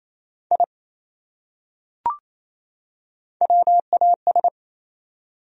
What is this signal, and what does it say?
Morse code, heard as a keyed tone, about 28 wpm, 700 Hz. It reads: I WAS
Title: Top 100 Words 28wpm